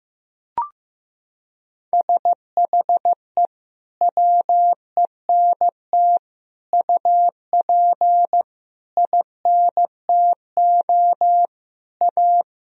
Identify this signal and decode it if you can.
Morse code, heard as a keyed tone, about 16 wpm, 700 Hz. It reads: SHE WENT UP INTO A
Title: Sentences from Top 1000 Words - Review Part 1 - 15wpm